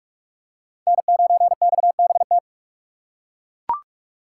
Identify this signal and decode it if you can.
Morse code, heard as a keyed tone, about 41 wpm, 700 Hz. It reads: N9XBT